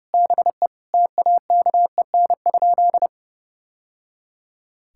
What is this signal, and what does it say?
Morse code, heard as a keyed tone, about 32 wpm, 700 Hz. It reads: BE TAKEN?
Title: Sentences from Top 1000 Words - Review Part 6 - 30wpm